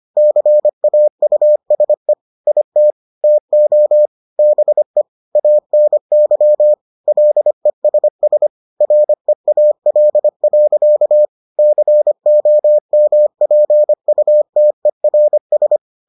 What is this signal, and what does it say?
Morse code, heard as a keyed tone, about 25 wpm, 600 Hz. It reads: CAUSE IT TO BE ANY LESS REAL. COMPUTERS